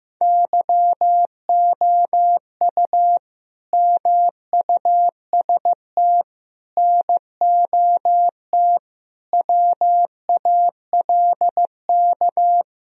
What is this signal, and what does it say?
Morse code, heard as a keyed tone, about 15 wpm, 700 Hz. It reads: YOU MUST NOT WALK
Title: Sentences from Top 1000 Words - Review Part 1 - 15wpm